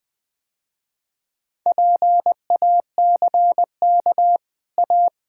Morse code, heard as a keyed tone, about 20 wpm, 700 Hz. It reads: PACK A
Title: Easy Sentences with Hints - 20wpm